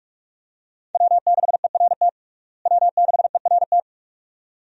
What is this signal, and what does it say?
Morse code, heard as a keyed tone, about 46 wpm, 700 Hz. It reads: W6ERT W6ERT
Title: USA Call Signs - 5 Characters - Repeated 2x - 45wpm